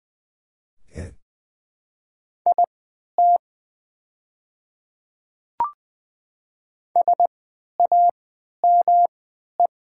Morse code, heard as a keyed tone, about 20 wpm, spaced as for 11 wpm, 700 Hz. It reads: IT SAME
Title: 3x Character Spacing - Top 200 Words - 20wpm